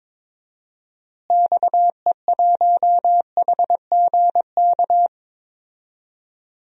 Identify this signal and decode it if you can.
Morse code, heard as a keyed tone, about 22 wpm, 700 Hz. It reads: XE1HGK
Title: DXCC-100 Call Signs - 6 Characters - Repeated 3x - 22wpm